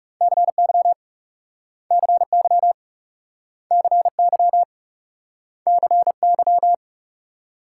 Morse code, heard as a keyed tone, about 38 wpm, 700 Hz. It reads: CY CY CY CY